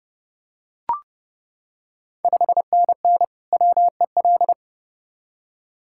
Morse code, heard as a keyed tone, about 30 wpm, 700 Hz. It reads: 5NN WEL
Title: Ontario QSO Party Exchange 30wpm